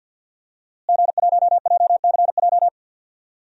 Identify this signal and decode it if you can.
Morse code, heard as a keyed tone, about 51 wpm, 700 Hz. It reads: G1JCJ
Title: DXCC-100 Call Signs - 5 Characters - Repeated 3x - 50wpm